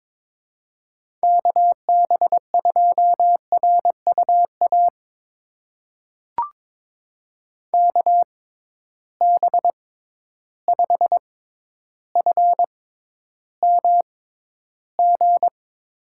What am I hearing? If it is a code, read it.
Morse code, heard as a keyed tone, about 22 wpm, spaced as for 11 wpm, 700 Hz. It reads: KB2RUA KB5FMG
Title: USA Call Signs - 6 Characters - 6x Character Spacing - 22wpm